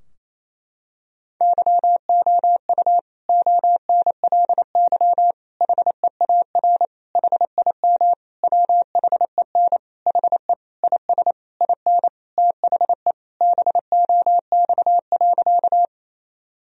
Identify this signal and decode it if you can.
Morse code, heard as a keyed tone, about 28 wpm, 700 Hz. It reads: YOU ONLY HEAR HIM WHEN HE IS IN THE BOX.
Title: Sentences from Top 1000 Words - Review Part 2 - 28wpm